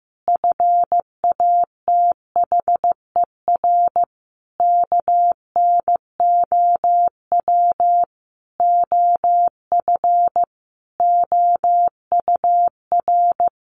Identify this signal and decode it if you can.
Morse code, heard as a keyed tone, about 14 wpm, 700 Hz. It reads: FATHER KNOW OF OUR
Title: Sentences from Top 1500 Words 15wpm